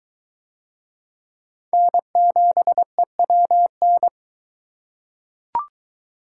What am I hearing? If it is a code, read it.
Morse code, heard as a keyed tone, about 22 wpm, 700 Hz. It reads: N7EWN